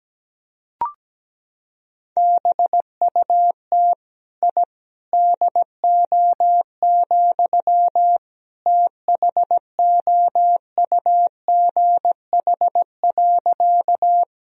Morse code, heard as a keyed tone, about 17 wpm, 700 Hz. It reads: BUT I DO, THOUGH.
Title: Sentences from Top 1000 Words - Review Part 2 - 17wpm